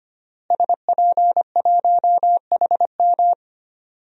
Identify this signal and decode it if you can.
Morse code, heard as a keyed tone, about 25 wpm, 700 Hz. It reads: SP1HM
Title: DXCC-100 Call Signs - 5 Characters - Repeated 2x - 25wpm